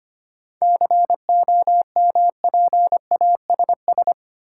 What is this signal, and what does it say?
Morse code, heard as a keyed tone, about 25 wpm, 690 Hz. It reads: COMPASS
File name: practice / 25wpm / Easy Words with Hints - 25wpm